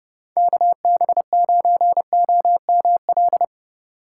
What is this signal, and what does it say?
Morse code, heard as a keyed tone, about 37 wpm, 700 Hz. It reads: KB9OML
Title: Speed Racing - USA 2x3 Call Signs 30wpm